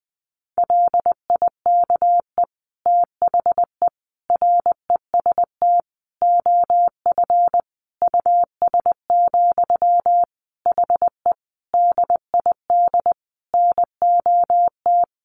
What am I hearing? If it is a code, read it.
Morse code, heard as a keyed tone, about 20 wpm, 700 Hz. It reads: LIKE THE REST OF US, HE DID NOT
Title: Sentences from Top 1400 Words 20wpm